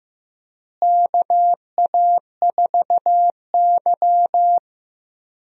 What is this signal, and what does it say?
Morse code, heard as a keyed tone, about 16 wpm, 700 Hz. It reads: KA4Y
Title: USA Call Signs - 4 Characters - Repeated 2x - 15wpm